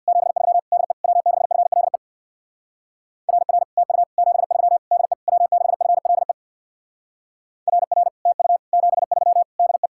Morse code, heard as a keyed tone, about 56 wpm, 700 Hz. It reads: 73 DE W6FLE RR TU 73 DE W6FLE RR TU 73 DE
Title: Speed-Racing Common QSO Phrases Encore - 50wpm